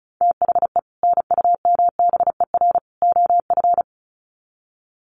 Morse code, heard as a keyed tone, about 34 wpm, 700 Hz. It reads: THE NUMBER OF
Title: Sets of 3 Words 35wpm